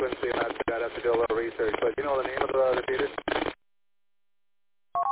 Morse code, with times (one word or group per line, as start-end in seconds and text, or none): none